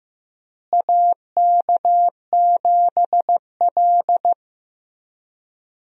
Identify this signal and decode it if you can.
Morse code, heard as a keyed tone, about 15 wpm, 700 Hz. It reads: AK7L
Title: USA Call Signs - 4 Characters - Repeated 3x - 15wpm